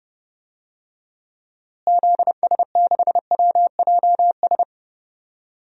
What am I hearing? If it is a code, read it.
Morse code, heard as a keyed tone, about 30 wpm, 700 Hz. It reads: ZS6WJS
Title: DXCC-100 Call Signs - 6 Characters - Repeated 3x - 30wpm